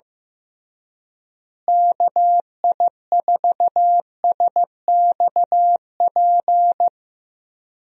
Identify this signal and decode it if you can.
Morse code, heard as a keyed tone, about 15 wpm, 700 Hz. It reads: KI4SXP
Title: USA Call Signs - 6 Characters - Repeated 3x - 15wpm